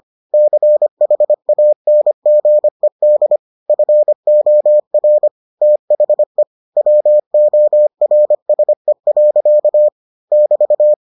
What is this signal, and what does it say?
Morse code, heard as a keyed tone, about 25 wpm, 600 Hz. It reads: CHANGED FOR THE WORSE. <BT>